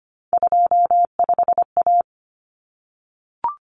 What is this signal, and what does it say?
Morse code, heard as a keyed tone, about 25 wpm, 700 Hz. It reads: 25A
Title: Common Cut Numbers - 3-Digit 25wpm